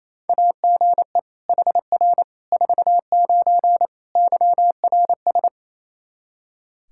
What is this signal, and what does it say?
Morse code, heard as a keyed tone, about 27 wpm, 700 Hz. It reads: AGE HR 49 YRS
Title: K1USN SST QSO Practice 28wpm